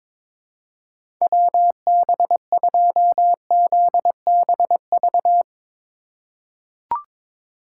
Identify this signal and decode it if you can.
Morse code, heard as a keyed tone, about 21 wpm, 700 Hz. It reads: WB2ZBV
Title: USA Call Signs - 6 Characters - 2x Character Spacing - 22wpm